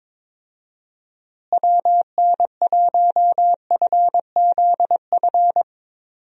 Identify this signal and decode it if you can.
Morse code, heard as a keyed tone, about 23 wpm, 700 Hz. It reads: WN1FZF
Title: USA Call Signs - 6 Characters - Repeated 2x - 22wpm